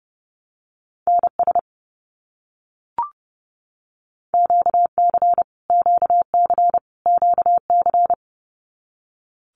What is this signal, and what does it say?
Morse code, heard as a keyed tone, about 30 wpm, 700 Hz. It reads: NS QC QC QC
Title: Canadian Provinces and Territories - Repeated 3x - 30wpm